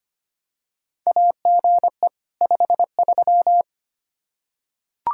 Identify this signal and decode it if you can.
Morse code, heard as a keyed tone, about 25 wpm, 700 Hz. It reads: AGE 53
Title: K1USN SST QSO Practice 25wpm